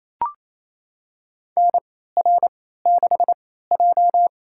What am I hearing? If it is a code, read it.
Morse code, heard as a keyed tone, about 28 wpm, spaced as for 16 wpm, 700 Hz. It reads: NR6J
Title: USA Call Signs - 4 Characters - 3x Character Spacing - 28wpm